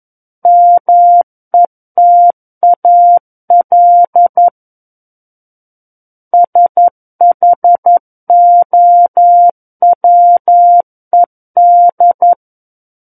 Morse code, heard as a keyed tone, about 11 wpm, 700 Hz. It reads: METAL SHOWED
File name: Book One / 02-Chapter 2 - 11 WPM